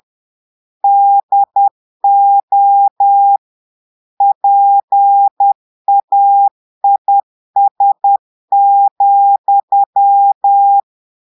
Morse code, heard as a keyed tone, about 10 wpm, 800 Hz. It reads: DO PAIS,
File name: Portugais - texte en clair 1 - vitesse - 10 mots minute